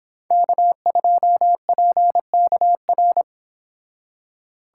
Morse code, heard as a keyed tone, about 25 wpm, 700 Hz. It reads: K2PKR